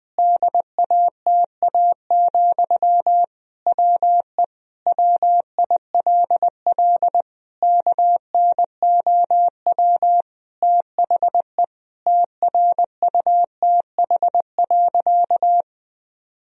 Morse code, heard as a keyed tone, about 20 wpm, 700 Hz. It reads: DATA, WE WILL KNOW THE TRUTH.
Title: Sentences from Top 1000 Words - Review Part 3 - 20wpm